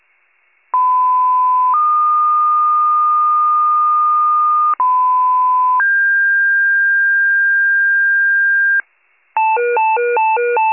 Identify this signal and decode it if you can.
Two-tone page, heard as a one-second tone, then a three-second tone, about 1010 then 1220 Hz, about 1010 then 1600 Hz, with noise. It is heard from 0.7 to 8.8 s.